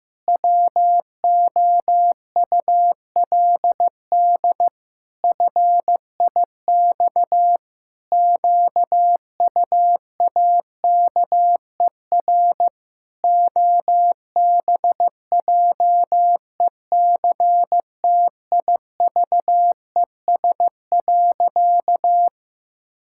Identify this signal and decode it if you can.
Morse code, heard as a keyed tone, about 15 wpm, 700 Hz. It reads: WOULD FIX QUAKER OBJECTIVES.
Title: Pangram Challenge v2 - 15wpm